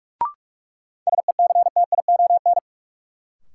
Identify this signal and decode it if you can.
Morse code, heard as a keyed tone, about 43 wpm, 700 Hz. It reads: SECTION
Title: Sentences from Top 1000 Words - Review Part 5 - New Word Drill - 45wpm